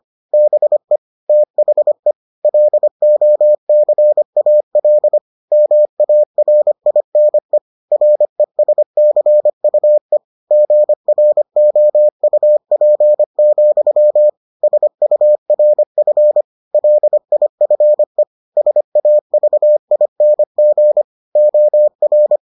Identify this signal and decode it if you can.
Morse code, heard as a keyed tone, about 25 wpm, 600 Hz. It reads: BE THE LOCAL MARINE RESCUE GROUP, SURF LIFE SAVING OR